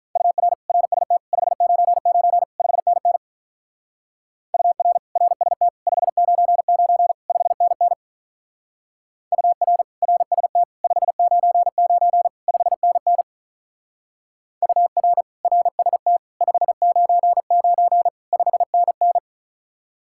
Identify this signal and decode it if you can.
Morse code, heard as a keyed tone, about 45 wpm, 700 Hz. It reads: UR RST 599 5NN UR RST 599 5NN UR RST 599 5NN UR RST 599 5NN